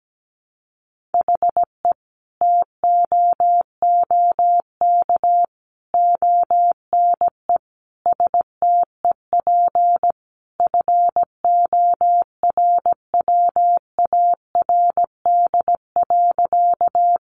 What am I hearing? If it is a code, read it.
Morse code, heard as a keyed tone, about 17 wpm, 700 Hz. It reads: HE TOOK ONE STEP FORWARD.
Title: Sentences from Top 1300 Words 17wpm